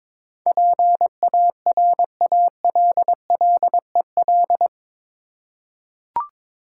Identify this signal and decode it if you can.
Morse code, heard as a keyed tone, about 21 wpm, 700 Hz. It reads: PARALLEL
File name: practice / 22wpm / Eight Letter Words - 22wpm